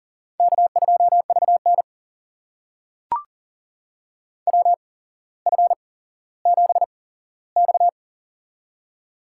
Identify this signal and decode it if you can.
Morse code, heard as a keyed tone, about 38 wpm, spaced as for 17 wpm, 700 Hz. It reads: K2VN WF7X
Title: USA Call Signs - 4 Characters - 8x Character Spacing - 40wpm